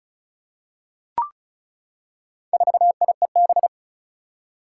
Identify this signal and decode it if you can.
Morse code, heard as a keyed tone, about 34 wpm, 700 Hz. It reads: 4IEB
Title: Four Challenging Letter-Number Combinations 35wpm